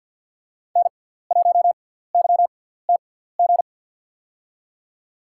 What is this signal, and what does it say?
Morse code, heard as a keyed tone, about 51 wpm, spaced as for 17 wpm, 690 Hz. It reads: N1YTG
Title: USA Call Signs - 5 Characters - 6x Character Spacing - 50wpm